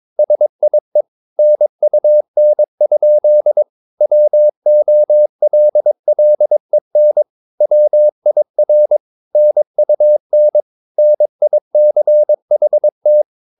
Morse code, heard as a keyed tone, about 22 wpm, 600 Hz. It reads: SIE NUN? WOLLEN WIR NUN NICHT